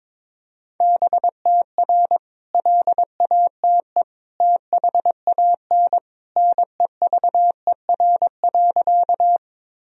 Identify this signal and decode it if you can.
Morse code, heard as a keyed tone, about 22 wpm, 700 Hz. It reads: BTR LATE THAN NEVER.
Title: Sentences from Top 400 Words and Abbreviations 22wpm